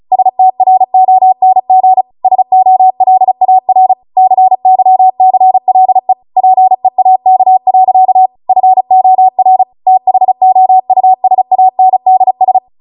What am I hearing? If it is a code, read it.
Morse code, heard as a keyed tone, about 36 wpm, 750 Hz. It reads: STRONG SOLAR CYCLE PEAK. FOR THOUSANDS